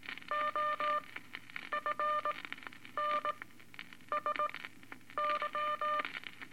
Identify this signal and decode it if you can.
Morse code, heard as a keyed tone, about 19 wpm, spaced as for 9 wpm, 1200 Hz, with noise. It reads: OFNSY
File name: Eight Words Per Minute